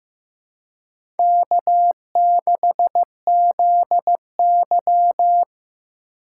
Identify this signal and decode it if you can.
Morse code, heard as a keyed tone, about 15 wpm, 700 Hz. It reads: K6ZY
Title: USA Call Signs - 4 Characters - 4x Character Spacing - 15wpm